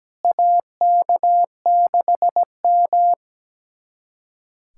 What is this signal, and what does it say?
Morse code, heard as a keyed tone, about 17 wpm, 700 Hz. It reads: AK6M
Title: USA Callsign DE Callsign - 4 Characters - 17wpm